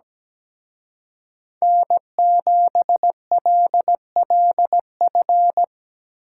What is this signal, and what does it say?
Morse code, heard as a keyed tone, about 19 wpm, 700 Hz. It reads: N7LLF
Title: Speed Racing - USA 1x3 Call Signs 17wpm